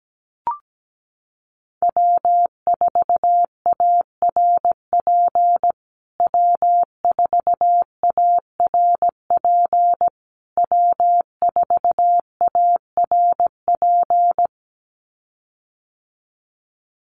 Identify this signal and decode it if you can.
Morse code, heard as a keyed tone, about 17 wpm, 700 Hz. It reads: W4ARP W4ARP W4ARP
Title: USA Call Signs Repeated 3x 17wpm